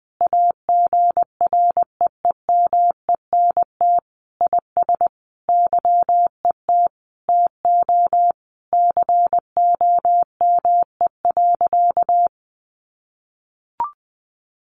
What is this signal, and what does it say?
Morse code, heard as a keyed tone, about 20 wpm, 700 Hz. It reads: AGREEMENT IS YET TO COME.
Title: Sentences from Top 1500 Words 20wpm